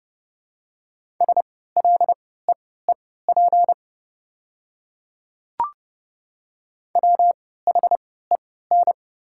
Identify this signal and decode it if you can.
Morse code, heard as a keyed tone, about 32 wpm, spaced as for 17 wpm, 700 Hz. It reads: SLEEP WHEN
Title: 3x Character Spacing - Top 500 Words - 30wpm